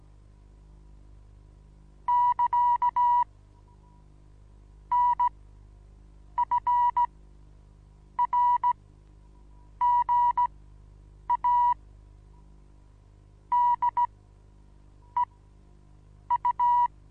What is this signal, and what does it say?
Morse code, heard as a keyed tone, about 16 wpm, spaced as for 8 wpm, 970 Hz. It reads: <KA> N F R GA D E U
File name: Onlinelehrgang / morsen / DJ4UF Morsen25